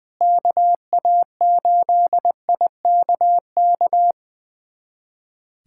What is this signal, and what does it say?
Morse code, heard as a keyed tone, about 20 wpm, 700 Hz. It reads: KA8IKK
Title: USA Callsign DE Callsign - 6 Characters - 20wpm